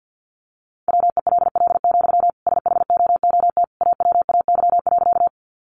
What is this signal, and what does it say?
Morse code, heard as a keyed tone, about 50 wpm, 700 Hz. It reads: WELL, SHOOT AWAY.
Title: Sentences from Top 1200 Words 50wpm